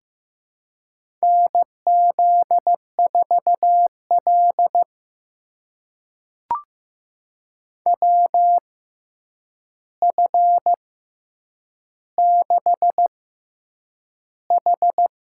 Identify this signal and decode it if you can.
Morse code, heard as a keyed tone, about 15 wpm, spaced as for 8 wpm, 700 Hz. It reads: NZ4L WF6H
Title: USA Call Signs - 6x Character Spacing - 15wpm